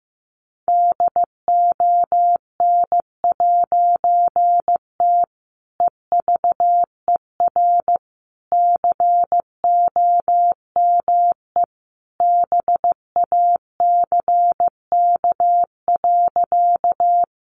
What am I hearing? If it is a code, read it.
Morse code, heard as a keyed tone, about 15 wpm, 700 Hz. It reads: DON'T EVER COME BACK.
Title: A Sound Of Thunder 15wpm